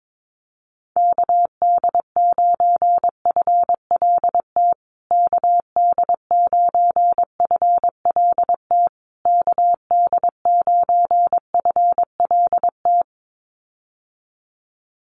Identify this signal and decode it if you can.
Morse code, heard as a keyed tone, about 22 wpm, 700 Hz. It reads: KD9FLT KD9FLT KD9FLT
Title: USA Call Signs Repeated 3x 22wpm